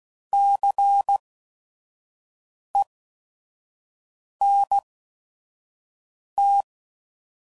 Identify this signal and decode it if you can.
Morse code, heard as a keyed tone, about 16 wpm, spaced as for 5 wpm, 800 Hz. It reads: CENT
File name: Prefixes / By First Letter / Cs